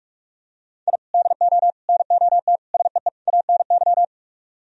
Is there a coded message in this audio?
Yes: Morse code from 0.9 to 4.0 s, heard as a keyed tone, about 44 wpm, 700 Hz. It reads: I DO NOT SEE ANY